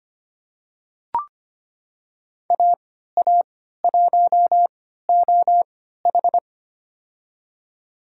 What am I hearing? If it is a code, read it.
Morse code, heard as a keyed tone, about 25 wpm, spaced as for 14 wpm, 700 Hz. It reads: AA1OH
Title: USA Call Signs - 5 Characters - 3x Character Spacing - 25wpm